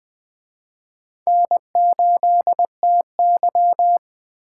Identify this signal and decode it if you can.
Morse code, heard as a keyed tone, about 21 wpm, 700 Hz. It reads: N8TY
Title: Speed Racing - USA 1x2 Call Signs 17wpm